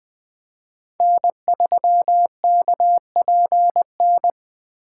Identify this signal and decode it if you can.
Morse code, heard as a keyed tone, about 20 wpm, 700 Hz. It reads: N3KPN
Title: USA Call Signs - 5 Characters - 2x Character Spacing - 20wpm